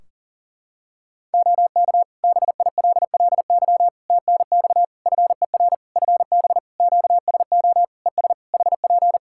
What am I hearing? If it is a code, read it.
Morse code, heard as a keyed tone, about 42 wpm, 700 Hz. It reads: OK BILLY TNX FER FB QSO ES HP
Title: Practice QSO - Exchange 4 - Ending - 40wpm